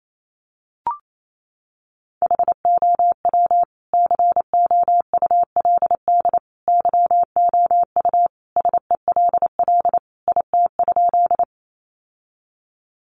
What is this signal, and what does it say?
Morse code, heard as a keyed tone, about 28 wpm, 700 Hz. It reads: HOW COULD YOU SELL IT?
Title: Sentences from Top 1100 Words 28wpm